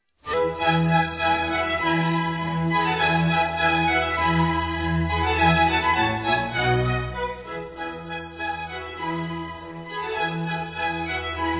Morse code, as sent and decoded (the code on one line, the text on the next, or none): none